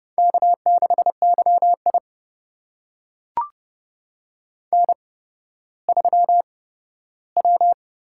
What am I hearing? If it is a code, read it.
Morse code, heard as a keyed tone, about 30 wpm, spaced as for 15 wpm, 700 Hz. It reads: K6YI N3W
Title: USA Call Signs - 8x Character Spacing - 30wpm